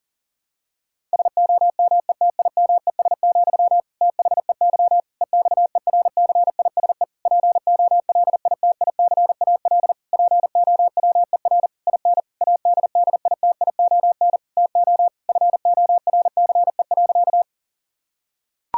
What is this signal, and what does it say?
Morse code, heard as a keyed tone, about 40 wpm, 700 Hz. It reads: SOMETIMES, THEY EXERCISE POLITICAL POWER IN ADDITION TO FORCE.
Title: Sentences from Top 1000 Words - Review Part 6 - 40wpm